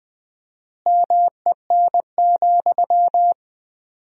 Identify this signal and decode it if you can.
Morse code, heard as a keyed tone, about 20 wpm, 700 Hz. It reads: MEN,
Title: 4x Word-Spacing - Sentences from Top 500 Words - 20wpm